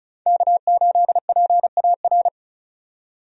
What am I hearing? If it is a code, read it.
Morse code, heard as a keyed tone, about 36 wpm, 700 Hz. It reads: K8PAR
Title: USA Call Signs - 5 Characters - 35wpm